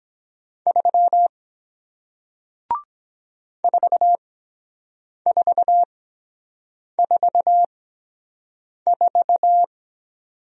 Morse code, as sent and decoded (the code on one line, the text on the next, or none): ...-- / ....- ....- ....- ....-
3 4444